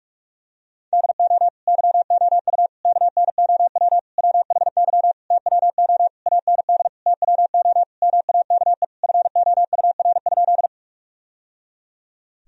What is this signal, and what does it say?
Morse code, heard as a keyed tone, about 46 wpm, 700 Hz. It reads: DO YOU KNOW WHY TWO AND TWO MAKE FOUR?
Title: Sentences from Top 1000 Words - Review Part 1 - 45wpm